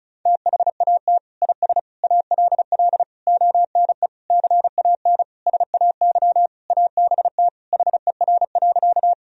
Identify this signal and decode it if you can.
Morse code, heard as a keyed tone, about 36 wpm, 700 Hz. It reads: THAT IS ALL ONE CAN SAY ABT HER.
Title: Sentences from Top 200 Words and Abbreviations 35wpm